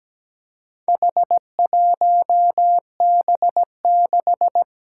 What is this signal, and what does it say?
Morse code, heard as a keyed tone, about 17 wpm, 700 Hz. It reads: H1B6
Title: Four Challenging Letter-Number Combinations 17wpm